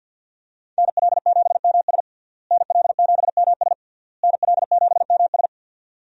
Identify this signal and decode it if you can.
Morse code, heard as a keyed tone, about 50 wpm, 700 Hz. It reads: NL7MS NL7MS NL7MS
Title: USA Call Signs - 5 Characters - Repeated 3x - 50wpm